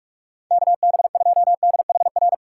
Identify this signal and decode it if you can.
Morse code, heard as a keyed tone, about 42 wpm, 700 Hz. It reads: KB2DHR
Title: USA Callsign DE Callsign - 6 Characters - 45wpm